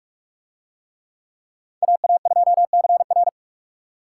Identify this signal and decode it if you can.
Morse code, heard as a keyed tone, about 44 wpm, 700 Hz. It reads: AA2CR